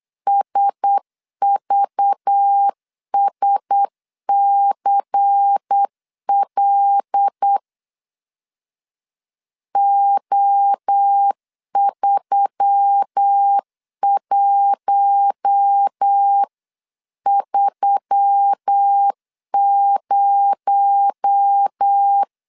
Morse code, heard as a keyed tone, about 8 wpm, 800 Hz. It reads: SVSCL O3130
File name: CW / DET / 1 6 1500C S B 8wpm 5142